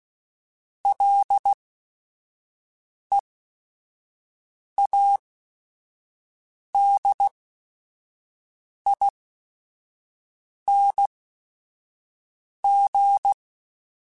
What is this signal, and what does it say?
Morse code, heard as a keyed tone, about 16 wpm, spaced as for 5 wpm, 800 Hz. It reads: LEADING
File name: Ls / Single L m3s / Single L @5